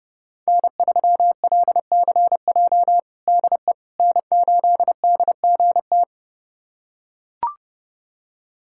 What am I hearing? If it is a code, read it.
Morse code, heard as a keyed tone, about 32 wpm, 700 Hz. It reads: N3LCJ DE N8DGT
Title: USA Callsign DE Callsign 30wpm